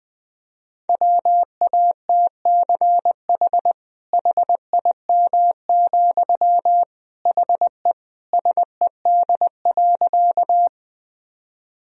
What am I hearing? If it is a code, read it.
Morse code, heard as a keyed tone, about 20 wpm, 700 Hz. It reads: WATCH HIM, HE SED.
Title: Sentences from Top 300 Words and Abbreviations 20wpm